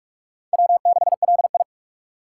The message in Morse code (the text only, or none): W6LI